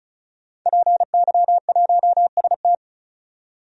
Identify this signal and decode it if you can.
Morse code, heard as a keyed tone, about 34 wpm, 700 Hz. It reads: PY1ST
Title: DXCC-100 Call Signs - 5 Characters - 35wpm